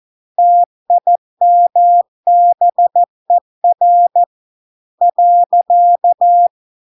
Morse code, heard as a keyed tone, about 14 wpm, 700 Hz. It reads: TIMBER .